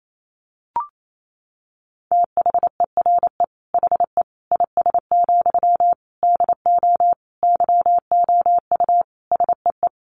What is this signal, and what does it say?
Morse code, heard as a keyed tone, about 28 wpm, 700 Hz. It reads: THERE HE IS, DO YOU SEE
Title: Sentences from Top 100 Words 28wpm